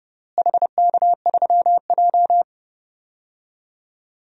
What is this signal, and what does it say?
Morse code, heard as a keyed tone, about 30 wpm, 700 Hz. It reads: HK3J
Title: DXCC-100 Call Signs - 4 Characters - Repeated 3x - 30wpm